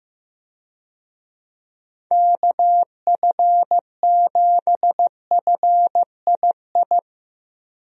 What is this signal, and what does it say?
Morse code, heard as a keyed tone, about 15 wpm, 700 Hz. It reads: KF7FII
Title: USA Call Signs - 6 Characters - Repeated 2x - 15wpm